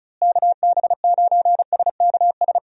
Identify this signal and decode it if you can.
Morse code, heard as a keyed tone, about 36 wpm, 700 Hz. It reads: KB9SKS
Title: USA Call Signs - 6 Characters - Repeated 2x - 35wpm